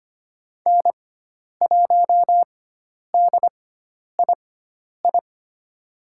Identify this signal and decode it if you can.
Morse code, heard as a keyed tone, about 25 wpm, spaced as for 10 wpm, 700 Hz. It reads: N1DII
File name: Lesson 16g - Letter c - Call Signs 25wpm